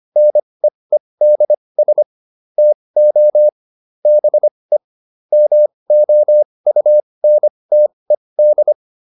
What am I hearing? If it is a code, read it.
Morse code, heard as a keyed tone, about 25 wpm, spaced as for 20 wpm, 600 Hz. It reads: NEEDS TO BE MOUNTED